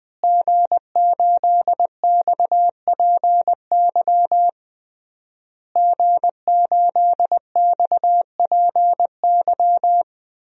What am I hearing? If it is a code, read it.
Morse code, heard as a keyed tone, about 20 wpm, 700 Hz. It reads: G8XPY G8XPY
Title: DXCC-100 Call Signs - 5 Characters - Repeated 3x - 20wpm